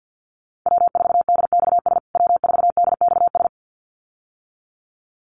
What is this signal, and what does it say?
Morse code, heard as a keyed tone, about 48 wpm, 700 Hz. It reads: W4DXS W4DXS
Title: USA Call Signs Repeated 2x 50wpm